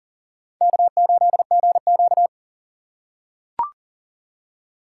Morse code, heard as a keyed tone, about 40 wpm, 700 Hz. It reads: K8GQ
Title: USA Call Signs - 4 Characters - 4x Character Spacing - 40wpm